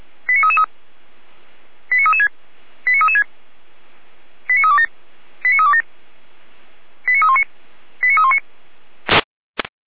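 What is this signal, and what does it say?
Five-tone selective call (ZVEI), heard as quick runs of five tones, tones of about 70 ms. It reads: 89333 89336 89336 89327 89327 89319 89319
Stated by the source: Five-tone ZVEI selective call received by SDR, recorded by feuerwehr-orlamuende fire brigade.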